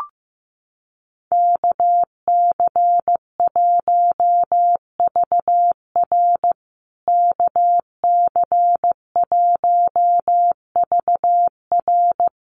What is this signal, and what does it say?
Morse code, heard as a keyed tone, about 15 wpm, 700 Hz. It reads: KC1VR KC1VR
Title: USA Call Signs Repeated 3x 15wpm